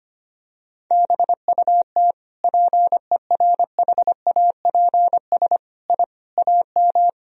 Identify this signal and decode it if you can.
Morse code, heard as a keyed tone, about 25 wpm, 700 Hz. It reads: BUT PERHAPS I AM
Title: Sentences from Top 1000 Words - Review Part 2 - 25wpm